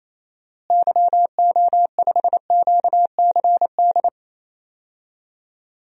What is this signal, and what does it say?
Morse code, heard as a keyed tone, about 28 wpm, 700 Hz. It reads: YO5QCD
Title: DXCC-100 Call Signs - 6 Characters - Repeated 2x - 28wpm